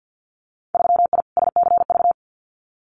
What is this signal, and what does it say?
Morse code, heard as a keyed tone, about 50 wpm, 700 Hz. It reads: 3I SCV
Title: Winter Field Day Exchanges Encore 50wpm